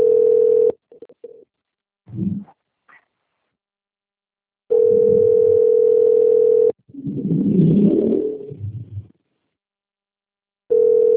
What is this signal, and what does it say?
Ringback tone (North America), 440 + 480 Hz.
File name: out- - - - - .65258